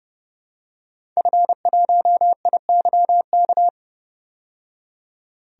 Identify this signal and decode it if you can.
Morse code, heard as a keyed tone, about 30 wpm, 700 Hz. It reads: F1IYK
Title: DXCC-100 Call Signs - 5 Characters - 30wpm